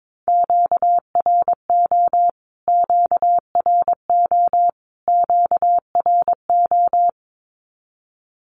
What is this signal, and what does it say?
Morse code, heard as a keyed tone, about 22 wpm, 700 Hz. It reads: QRO QRO QRO
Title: Q-Codes Repeated 3x 22wpm